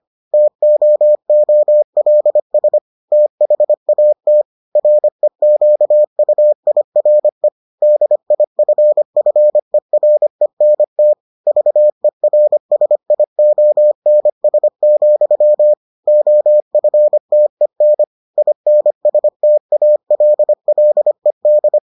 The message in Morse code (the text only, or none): TOOLS THAT REQUIRE DIFFERENT VERSIONS, OFTEN INSTALLED